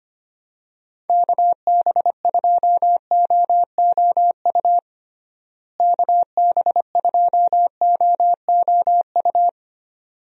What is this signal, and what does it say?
Morse code, heard as a keyed tone, about 25 wpm, 700 Hz. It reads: KB2OOU KB2OOU
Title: USA Call Signs - 6 Characters - Repeated 2x - 25wpm